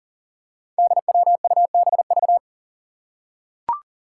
Morse code, heard as a keyed tone, about 38 wpm, 700 Hz. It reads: DWUBV
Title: Five Challenging Letter-Number Combinations 40wpm